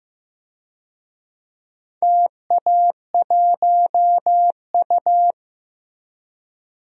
Morse code, heard as a keyed tone, about 15 wpm, 700 Hz. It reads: TA1U
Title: DXCC-100 Call Signs - 4 Characters - 15wpm